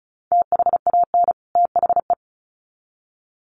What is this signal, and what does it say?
Morse code, heard as a keyed tone, about 34 wpm, 700 Hz. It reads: THAN THE
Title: Sets of 2 Words 35wpm